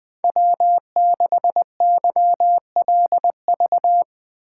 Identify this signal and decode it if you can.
Morse code, heard as a keyed tone, about 20 wpm, 700 Hz. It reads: W6YLV
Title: USA Call Signs - 5 Characters - 3x Character Spacing - 20wpm